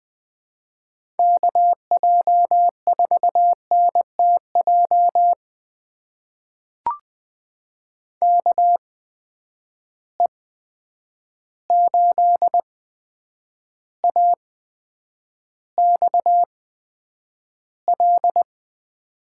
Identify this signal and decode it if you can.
Morse code, heard as a keyed tone, about 20 wpm, spaced as for 9 wpm, 700 Hz. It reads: KJ4NTJ KE8AXL
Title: USA Call Signs - 8x Character Spacing - 20wpm